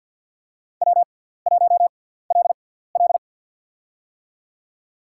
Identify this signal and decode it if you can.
Morse code, heard as a keyed tone, about 50 wpm, spaced as for 17 wpm, 700 Hz. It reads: W1LL
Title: USA Call Signs - 4 Characters - 6x Character Spacing - 50wpm